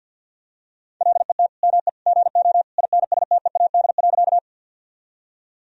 Morse code, heard as a keyed tone, about 50 wpm, 700 Hz. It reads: LET ME GO INSTEAD.
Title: Sentences from Top 1000 Words - Review Part 4 - 50wpm